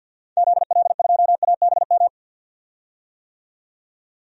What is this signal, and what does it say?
Morse code, heard as a keyed tone, about 52 wpm, 700 Hz. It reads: ZL2ABM
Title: DXCC-100 Call Signs - 6 Characters - 50wpm